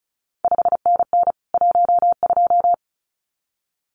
Morse code, heard as a keyed tone, about 36 wpm, 700 Hz. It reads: 5NN 12
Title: CQ WW DX Contest Exchange Encore - 35wpm